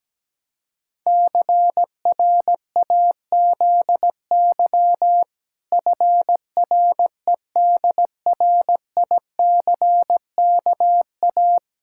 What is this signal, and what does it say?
Morse code, heard as a keyed tone, about 17 wpm, 700 Hz. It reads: CRAZY FREDRICKA
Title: Pangram Challenge v2 - 17wpm